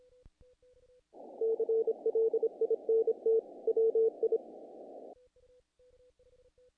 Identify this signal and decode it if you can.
Morse code, heard as a keyed tone, about 26 wpm, 450 Hz. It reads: CLINT WI
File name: contests / naqp-cw-k5tr-op-k5pi- -1800